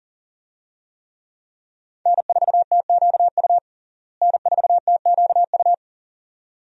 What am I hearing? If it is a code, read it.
Morse code, heard as a keyed tone, about 37 wpm, 700 Hz. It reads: N4TQU N4TQU
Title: USA Call Signs - 5 Characters - Repeated 2x - 40wpm